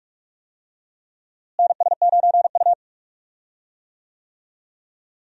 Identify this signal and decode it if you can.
Morse code, heard as a keyed tone, about 43 wpm, 700 Hz. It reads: NS9U
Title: USA Call Signs - 4 Characters - Repeated 2x - 45wpm